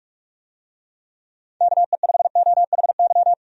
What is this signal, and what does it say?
Morse code, heard as a keyed tone, about 46 wpm, 700 Hz. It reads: KE5OHY
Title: USA Call Signs - 6 Characters - 45wpm